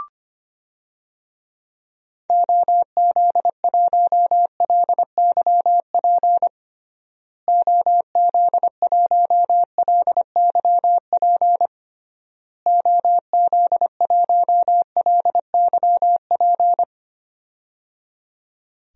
Morse code, heard as a keyed tone, about 25 wpm, 700 Hz. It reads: OZ1LYP OZ1LYP OZ1LYP
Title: DXCC-100 Call Signs - 6 Characters - Repeated 3x - 25wpm